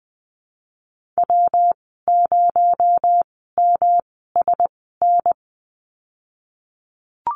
Morse code, heard as a keyed tone, about 20 wpm, spaced as for 15 wpm, 700 Hz. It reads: W0MSN
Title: 245 Lesson 39g - Letter z - Call Signs 20wpm